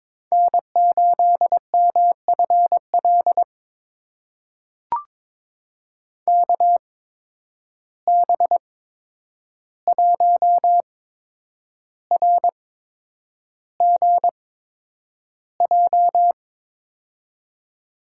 Morse code, heard as a keyed tone, about 22 wpm, spaced as for 9 wpm, 700 Hz. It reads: N8MFL KB1RGJ
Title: USA Call Signs - 8x Character Spacing - 22wpm